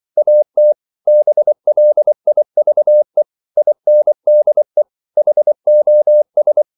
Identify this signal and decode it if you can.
Morse code, heard as a keyed tone, about 24 wpm, 600 Hz. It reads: AT BLIVE INDE HOS
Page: Ferie 24 wpm